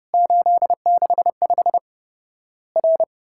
Morse code, heard as a keyed tone, about 30 wpm, 700 Hz. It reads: 865 R
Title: Worked All Europe DX Contest - QTC Exchange - 30wpm